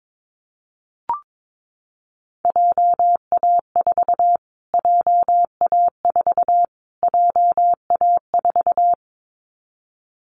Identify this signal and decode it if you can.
Morse code, heard as a keyed tone, about 22 wpm, 700 Hz. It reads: JA4 JA4 JA4
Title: Callsign Prefixes for Top 25 DXCC - Repeated 3x - 22wpm